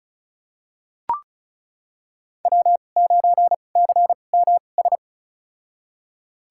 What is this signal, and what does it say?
Morse code, heard as a keyed tone, about 36 wpm, spaced as for 25 wpm, 700 Hz. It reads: W9CMS
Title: USA Call Signs - 5 Characters - 2x Character Spacing - 35wpm